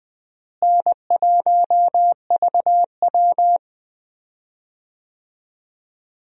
Morse code, heard as a keyed tone, about 20 wpm, 700 Hz. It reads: N1VW
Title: USA Call Signs - 4 Characters - Repeated 2x - 20wpm